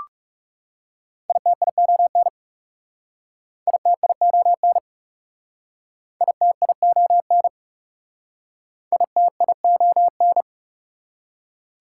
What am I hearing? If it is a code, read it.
Morse code, heard as a keyed tone, about 40 wpm, 690 Hz. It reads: ITION ITION ITION ITION